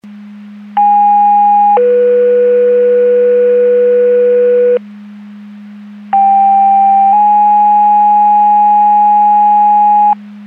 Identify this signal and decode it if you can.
Two-tone page, heard as a one-second tone, then a three-second tone, about 820 then 490 Hz, about 790 then 830 Hz. It is heard from 0.8 to 10.1 s.
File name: Portage County 154.3850 Fire EMS Paging